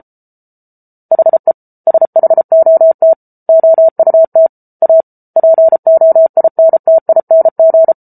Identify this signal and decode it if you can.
Morse code, heard as a keyed tone, about 34 wpm, 650 Hz. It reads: HE SHOT OUT A POINTING